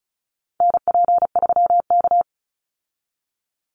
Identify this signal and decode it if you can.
Morse code, heard as a keyed tone, about 34 wpm, 700 Hz. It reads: NP3K
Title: Call Signs Repeated 3x - 35wpm